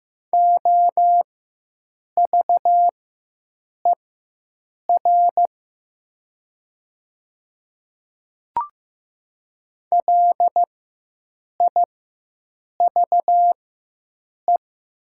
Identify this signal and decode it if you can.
Morse code, heard as a keyed tone, about 15 wpm, spaced as for 7 wpm, 700 Hz. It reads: OVER LIVE
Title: 4x Character Spacing - Top 200 Words - 15wpm